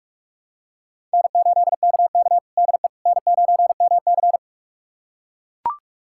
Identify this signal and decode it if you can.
Morse code, heard as a keyed tone, about 41 wpm, 700 Hz. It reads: N8KK DE N9MC
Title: USA Callsign DE Callsign - 4 Characters - 45wpm